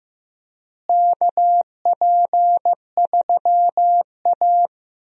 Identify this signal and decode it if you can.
Morse code, heard as a keyed tone, about 16 wpm, 700 Hz. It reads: KP3A
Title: USA Call Signs - 4 Characters - Repeated 2x - 15wpm